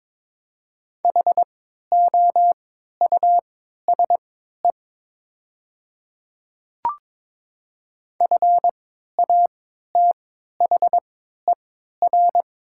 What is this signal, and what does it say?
Morse code, heard as a keyed tone, about 22 wpm, spaced as for 12 wpm, 700 Hz. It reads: HOUSE FATHER
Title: 3x Character Spacing - Top 500 Words - 22wpm